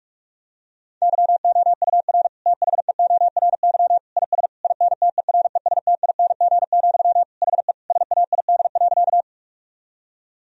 Morse code, heard as a keyed tone, about 41 wpm, 700 Hz. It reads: YOUR THEORY IS INTERESTING, HE SAID.
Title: Sentences from Top 1000 Words - Review Part 3 - 45wpm